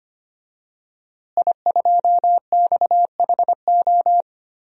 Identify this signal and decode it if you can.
Morse code, heard as a keyed tone, about 25 wpm, 700 Hz. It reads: I2XHO
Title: DXCC-100 Call Signs - 5 Characters - Repeated 3x - 25wpm